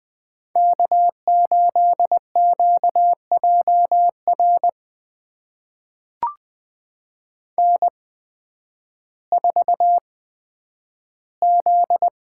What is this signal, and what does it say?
Morse code, heard as a keyed tone, about 20 wpm, spaced as for 11 wpm, 700 Hz. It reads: K8QJR N4Z
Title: USA Call Signs - 8x Character Spacing - 20wpm